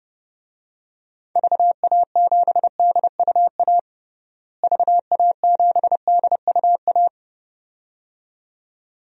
Morse code, heard as a keyed tone, about 32 wpm, 700 Hz. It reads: VA7DUA VA7DUA
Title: DXCC-100 Call Signs - 6 Characters - Repeated 2x - 30wpm